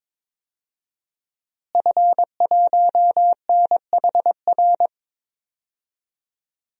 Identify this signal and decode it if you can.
Morse code, heard as a keyed tone, about 21 wpm, 700 Hz. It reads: F1NHR
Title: DXCC-100 Call Signs - 5 Characters - Repeated 2x - 22wpm